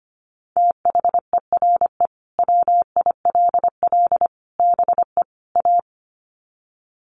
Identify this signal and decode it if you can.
Morse code, heard as a keyed tone, about 25 wpm, 700 Hz. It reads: THERE WILL BE A
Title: Sets of 4 Words 25wpm